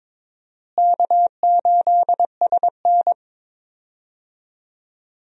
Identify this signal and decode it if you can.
Morse code, heard as a keyed tone, about 22 wpm, 700 Hz. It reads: K8SN
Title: USA Call Signs - 4 Characters - Repeated 2x - 22wpm